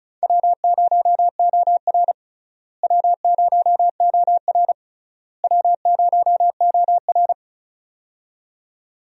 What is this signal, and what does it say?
Morse code, heard as a keyed tone, about 36 wpm, 700 Hz. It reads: W0OR W0OR W0OR
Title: USA Call Signs - 4 Characters - Repeated 3x - 35wpm